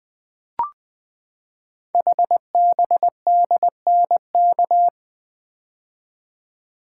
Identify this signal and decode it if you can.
Morse code, heard as a keyed tone, about 20 wpm, 700 Hz. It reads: HBDNK
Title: Code Groups - 5 Letters - 20wpm